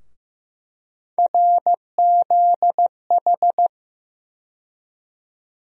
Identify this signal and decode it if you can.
Morse code, heard as a keyed tone, about 15 wpm, 700 Hz. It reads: RZH
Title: USA Call Sign Postfixes - 3 Letters 15wpm